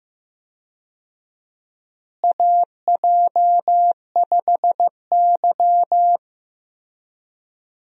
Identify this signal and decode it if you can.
Morse code, heard as a keyed tone, about 15 wpm, 700 Hz. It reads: AJ5Y
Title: USA Call Signs - 4 Characters - Repeated 3x - 15wpm